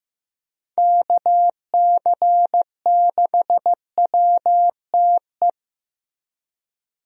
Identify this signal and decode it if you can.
Morse code, heard as a keyed tone, about 16 wpm, 700 Hz. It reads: KC6WTE
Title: USA Call Signs - 6 Characters - Repeated 2x - 15wpm